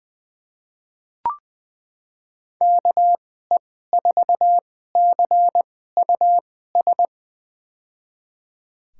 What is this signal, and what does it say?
Morse code, heard as a keyed tone, about 21 wpm, spaced as for 15 wpm, 700 Hz. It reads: KE4CUS
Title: USA Call Signs - 2x Character Spacing - 20wpm